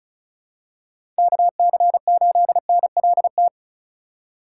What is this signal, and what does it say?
Morse code, heard as a keyed tone, about 36 wpm, 700 Hz. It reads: KC8NLT
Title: USA Call Signs - 6 Characters - 2x Character Spacing - 35wpm